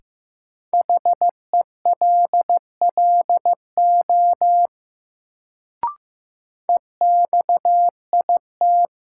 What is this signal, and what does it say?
Morse code, heard as a keyed tone, about 15 wpm, 700 Hz. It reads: HELLO EXIT
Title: Sentences from Top 1100 Words - New Word Drill - 15wpm